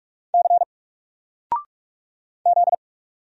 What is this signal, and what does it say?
Morse code, heard as a keyed tone, about 44 wpm, 700 Hz. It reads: C Z
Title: Single Character (Letter:Number:Punctuation:Prosign) 45wpm